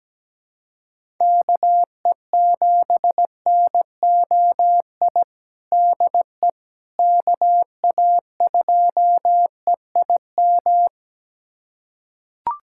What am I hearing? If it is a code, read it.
Morse code, heard as a keyed tone, about 17 wpm, 700 Hz. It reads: KE7NOI DE KA2EIM
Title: USA Callsign DE Callsign - 6 Characters - 17wpm